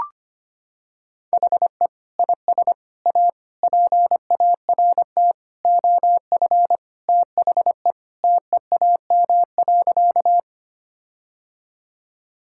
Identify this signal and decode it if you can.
Morse code, heard as a keyed tone, about 25 wpm, 700 Hz. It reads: HE IS A PART OF THE TEAM.
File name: practice / 25wpm / Sentences with 4-Word Phrases - 25wpm